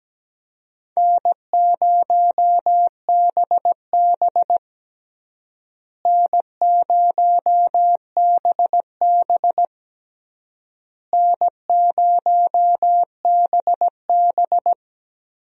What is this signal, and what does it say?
Morse code, heard as a keyed tone, about 17 wpm, 700 Hz. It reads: N0BB N0BB N0BB
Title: USA Call Signs - 4 Characters - Repeated 3x - 17wpm